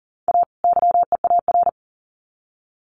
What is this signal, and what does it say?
Morse code, heard as a keyed tone, about 40 wpm, 700 Hz. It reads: A YEAR
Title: Sets of 2 Words 40wpm